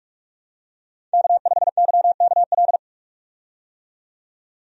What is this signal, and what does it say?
Morse code, heard as a keyed tone, about 46 wpm, 700 Hz. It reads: K5YKL
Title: USA Call Signs - 5 Characters - 45wpm